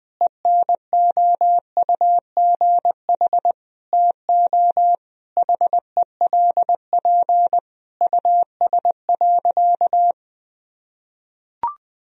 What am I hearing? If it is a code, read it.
Morse code, heard as a keyed tone, about 20 wpm, 700 Hz. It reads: ENOUGH TO HELP US.
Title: Sentences from Top 1000 Words - Review Part 3 - 20wpm